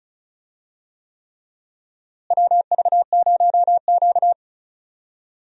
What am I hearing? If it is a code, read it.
Morse code, heard as a keyed tone, about 36 wpm, 700 Hz. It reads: WV0Q
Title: USA Call Signs - 4 Characters - Repeated 3x - 35wpm